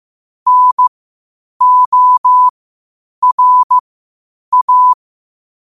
Morse code, heard as a keyed tone, about 15 wpm, spaced as for 8 wpm, 1000 Hz. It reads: NORA
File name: qotdfiles / 8wpm